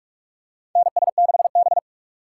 Morse code, heard as a keyed tone, about 46 wpm, 700 Hz. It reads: NS6B
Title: USA Call Signs - 4 Characters - 8x Character Spacing - 45wpm